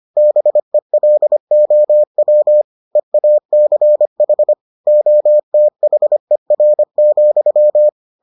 Morse code, heard as a keyed tone, about 25 wpm, 600 Hz. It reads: BELOW EACH OTHER,